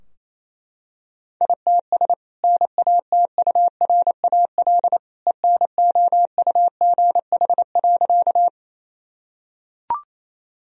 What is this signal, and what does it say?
Morse code, heard as a keyed tone, about 28 wpm, 700 Hz. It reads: ITS NATURAL ENOUGH.
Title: Sentences from Top 1000 Words - Review Part 4 - 28wpm